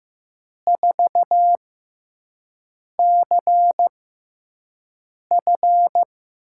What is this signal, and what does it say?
Morse code, heard as a keyed tone, about 15 wpm, spaced as for 5 wpm, 700 Hz. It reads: 4CF
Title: USA Call Signs - 6 Characters - 6x Character Spacing - 15wpm